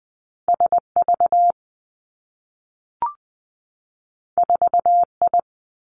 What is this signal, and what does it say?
Morse code, heard as a keyed tone, about 20 wpm, 700 Hz. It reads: S V 4 I
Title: Two Challenging Letter-Number Combinations 20wpm